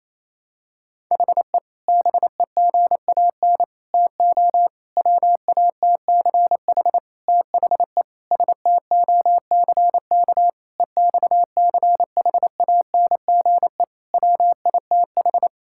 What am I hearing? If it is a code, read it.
Morse code, heard as a keyed tone, about 28 wpm, 700 Hz. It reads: HE BEGAN TO WATCH THE STOCK EXCHANGE WITH
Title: Sentences from Top 1000 Words - Review Part 6 - 28wpm